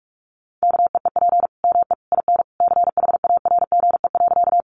Morse code, heard as a keyed tone, about 44 wpm, 700 Hz. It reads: KEEP ME IN CHARGE.
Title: Sentences from Top 900 Words 45wpm